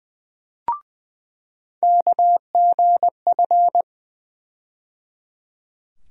Morse code, heard as a keyed tone, about 20 wpm, 700 Hz. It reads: KGF
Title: USA Call Sign Postfixes - 3 Letters 20wpm